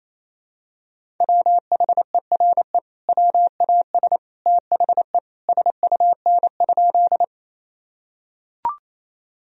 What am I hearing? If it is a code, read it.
Morse code, heard as a keyed tone, about 28 wpm, 700 Hz. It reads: WHERE WAS THE SUN?
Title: Sentences from Top 1000 Words - Review Part 1 - 28wpm